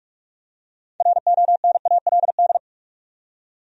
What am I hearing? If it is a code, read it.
Morse code, heard as a keyed tone, about 46 wpm, 700 Hz. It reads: RONALD
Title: USA Common Names - 45wpm